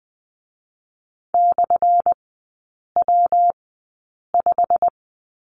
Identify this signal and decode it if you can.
Morse code, heard as a keyed tone, about 20 wpm, 700 Hz. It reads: / W 5
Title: 218 Lesson 35e - Number 0 - Three Characters 20wpm